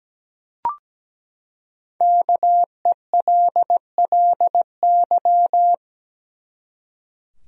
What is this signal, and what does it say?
Morse code, heard as a keyed tone, about 17 wpm, 700 Hz. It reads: KELLY
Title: USA Common Names - Max 5 Letters - 17wpm